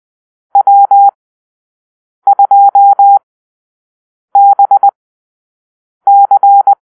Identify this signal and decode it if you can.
Morse code, heard as a keyed tone, about 20 wpm, spaced as for 6 wpm, 800 Hz. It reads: W2BC
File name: arrl / Call-Sign Drill-6